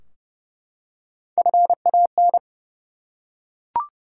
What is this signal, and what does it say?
Morse code, heard as a keyed tone, about 29 wpm, 700 Hz. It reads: FAN
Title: Three Letter Words - 30wpm